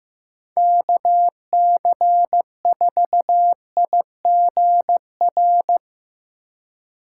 Morse code, heard as a keyed tone, about 15 wpm, 700 Hz. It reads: KC4IGR
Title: USA Call Signs - 3x Character Spacing - 15wpm